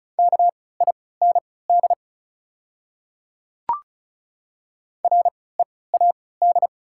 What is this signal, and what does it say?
Morse code, heard as a keyed tone, about 36 wpm, spaced as for 20 wpm, 700 Hz. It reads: KIND READ
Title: 3x Character Spacing - Top 200 Words - 35wpm